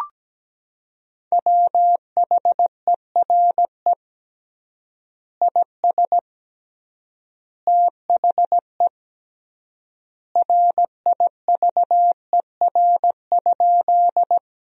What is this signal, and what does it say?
Morse code, heard as a keyed tone, about 17 wpm, 700 Hz. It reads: WHERE IS THE RIVER?
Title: 3x Word-Spacing - Sentences from Top 300 Words - 17wpm